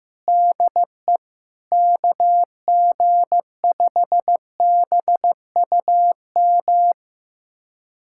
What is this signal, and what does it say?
Morse code, heard as a keyed tone, about 15 wpm, 700 Hz. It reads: DE KG5BUM
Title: USA Callsign DE Callsign - 6 Characters - 15wpm